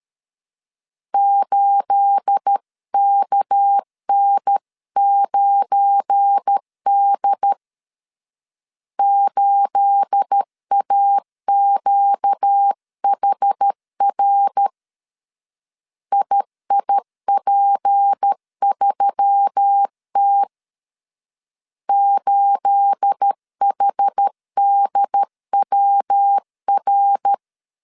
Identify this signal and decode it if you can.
Morse code, heard as a keyed tone, about 13 wpm, 800 Hz. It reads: 8KN9D 8AQHR IIP3T 8HDWR
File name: CW / DET / 1 9 1500C S A 12wpm 3800